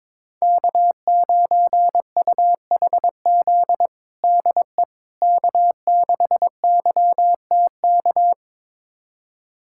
Morse code, heard as a keyed tone, about 22 wpm, 700 Hz. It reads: K9UHZ DE K6YTK
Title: USA Callsign DE Callsign - 5 Characters - 22wpm